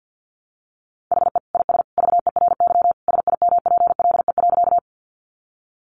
Morse code, heard as a keyed tone, about 50 wpm, 700 Hz. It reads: HE IS VERY SIMPLE.
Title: Sentences from Top 400 Words 50wpm